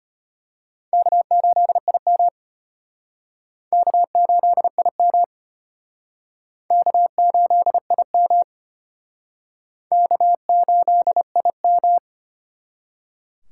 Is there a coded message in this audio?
Yes: Morse code from 0.9 to 12.0 s, heard as a keyed tone, about 33 wpm, 700 Hz. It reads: K8IM K8IM K8IM K8IM